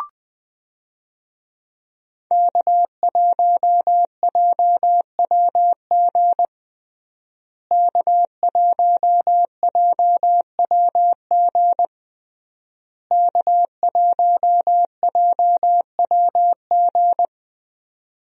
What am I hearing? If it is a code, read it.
Morse code, heard as a keyed tone, about 21 wpm, 700 Hz. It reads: K1JWG K1JWG K1JWG
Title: USA Call Signs - 5 Characters - Repeated 3x - 20wpm